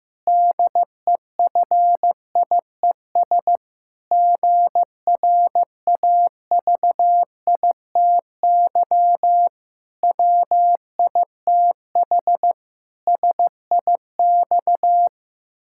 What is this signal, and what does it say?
Morse code, heard as a keyed tone, about 16 wpm, 700 Hz. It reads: DEFIES GRAVITY WITH SIX
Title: Pangram Challenge v2 - 15wpm